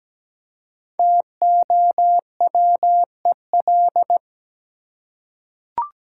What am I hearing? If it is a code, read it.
Morse code, heard as a keyed tone, about 17 wpm, 700 Hz. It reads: TOWEL
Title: Five Letter Words - 17wpm